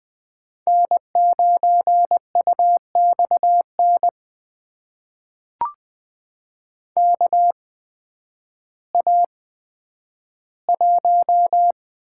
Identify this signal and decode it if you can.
Morse code, heard as a keyed tone, about 20 wpm, spaced as for 11 wpm, 700 Hz. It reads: N9UXN KA1
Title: USA Call Signs - 8x Character Spacing - 20wpm